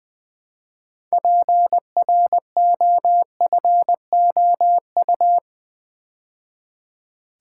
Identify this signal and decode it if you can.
Morse code, heard as a keyed tone, about 20 wpm, 700 Hz. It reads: PROFOU
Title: Build A Word - 2 Letters At A Time - 8-Letter Words - 20wpm